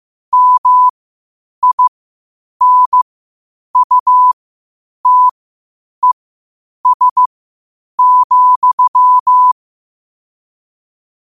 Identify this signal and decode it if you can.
Morse code, heard as a keyed tone, about 15 wpm, spaced as for 8 wpm, 1000 Hz. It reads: MINUTES,
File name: qotdfiles / 8wpm